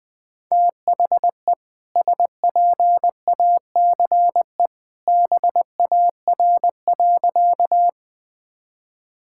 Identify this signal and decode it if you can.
Morse code, heard as a keyed tone, about 20 wpm, 700 Hz. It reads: THE SPACE BAR.
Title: All-Age Appropriate Jokes - 20wpm